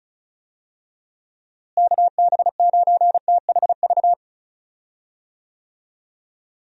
Morse code, heard as a keyed tone, about 36 wpm, 700 Hz. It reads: KB9THV
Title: USA Call Signs - 6 Characters - Repeated 2x - 35wpm